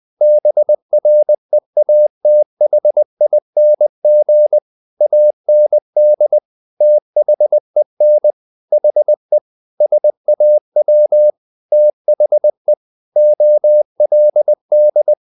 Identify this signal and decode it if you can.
Morse code, heard as a keyed tone, about 20 wpm, 600 Hz. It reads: BREATHING AND THEN HE SAW THE OLD
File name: the old man and the sea part 2 20wpm